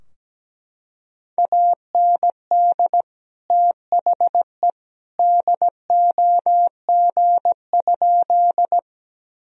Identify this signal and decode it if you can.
Morse code, heard as a keyed tone, about 17 wpm, 700 Hz. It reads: AND THE DOG?
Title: Sentences from Top 1000 Words - Review Part 2 - 17wpm